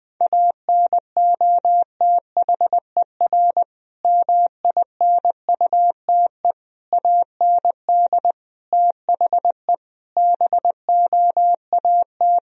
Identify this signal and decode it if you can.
Morse code, heard as a keyed tone, about 20 wpm, 700 Hz. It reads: ANOTHER MINUTE AND THE BOAT
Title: Sentences from Top 1000 Words - Review Part 3 - 20wpm